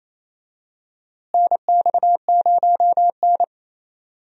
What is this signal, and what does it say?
Morse code, heard as a keyed tone, about 28 wpm, 700 Hz. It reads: NX0N
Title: USA Call Signs - 4 Characters - Repeated 2x - 28wpm